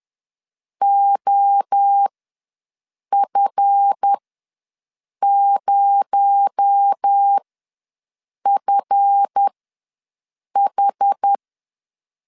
Morse code, heard as a keyed tone, about 10 wpm, spaced as for 6 wpm, 800 Hz. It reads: OF0FH
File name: LEZIONI / 7 GF9Z / 7 GF ws 10wpm 3918